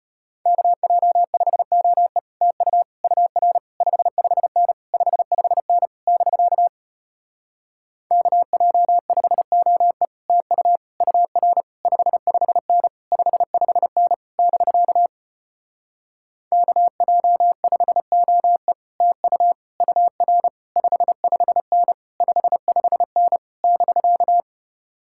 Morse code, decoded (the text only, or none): KJ5OE TU UR 55N 55N <BK> KJ5OE TU UR 55N 55N <BK> KJ5OE TU UR 55N 55N <BK>